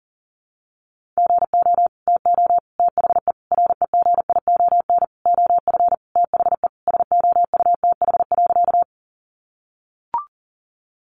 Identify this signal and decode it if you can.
Morse code, heard as a keyed tone, about 40 wpm, 700 Hz. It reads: GO TO THE REGION OF THE SOUTH.
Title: Sentences from Top 1500 Words 40wpm